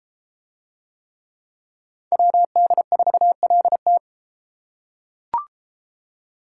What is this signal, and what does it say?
Morse code, heard as a keyed tone, about 31 wpm, 700 Hz. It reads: WD4LT